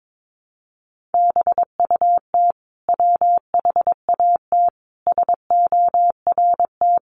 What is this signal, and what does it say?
Morse code, heard as a keyed tone, about 22 wpm, 700 Hz. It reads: BUT WHAT SORT
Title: Sentences from Top 1300 Words 22wpm